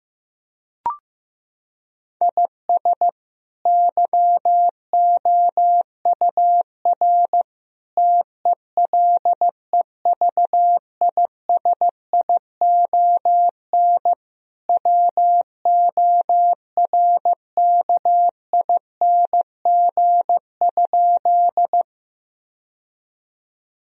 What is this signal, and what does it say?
Morse code, heard as a keyed tone, about 16 wpm, 700 Hz. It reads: IS YOUR TELEVISION WORKING?
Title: Sentences from Top 1000 Words - Review Part 4 - 15wpm